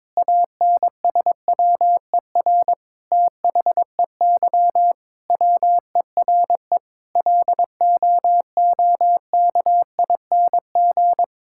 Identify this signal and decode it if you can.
Morse code, heard as a keyed tone, about 22 wpm, 700 Hz. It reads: ANSWER THEY WERE LOOKING
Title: Sentences from Top 1000 Words - Review Part 6 - 22wpm